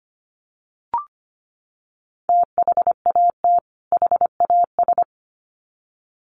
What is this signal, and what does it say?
Morse code, heard as a keyed tone, about 25 wpm, 700 Hz. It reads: THAT HAS
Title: Sets of 2 Words 25wpm